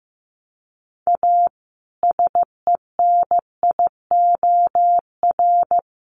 Morse code, heard as a keyed tone, about 15 wpm, 700 Hz. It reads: A SENIOR
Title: Sets of 5 Words Encore 15wpm